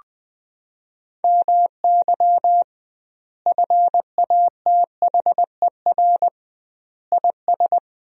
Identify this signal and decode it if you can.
Morse code, heard as a keyed tone, about 20 wpm, 700 Hz. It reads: MY FATHER IS
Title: 2x Word-Spacing - Sentences from Top 300 Words - 20wpm